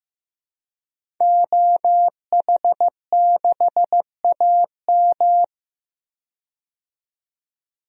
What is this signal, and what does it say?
Morse code, heard as a keyed tone, about 15 wpm, 700 Hz. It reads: OH6AM
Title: DXCC-100 Call Signs - 5 Characters - 15wpm